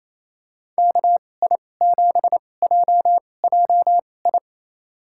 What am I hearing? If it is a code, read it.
Morse code, heard as a keyed tone, about 28 wpm, spaced as for 20 wpm, 700 Hz. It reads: KI7JJI
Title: USA Call Signs - 6 Characters - 2x Character Spacing - 28wpm